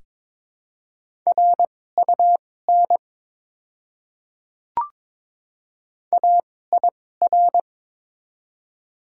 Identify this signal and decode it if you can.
Morse code, heard as a keyed tone, about 23 wpm, spaced as for 16 wpm, 700 Hz. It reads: RUN AIR
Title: 2x Character Spacing - Top 300 Words - 22wpm